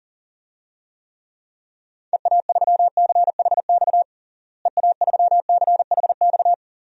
Morse code, heard as a keyed tone, about 40 wpm, 700 Hz. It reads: EA3CHX EA3CHX
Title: DXCC-100 Call Signs - 6 Characters - Repeated 2x - 40wpm